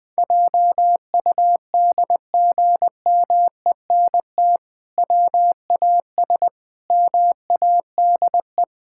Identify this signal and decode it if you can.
Morse code, heard as a keyed tone, about 20 wpm, 700 Hz. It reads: JUDGMENT WAS MADE
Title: Sentences with 5-Word Phrases - 20wpm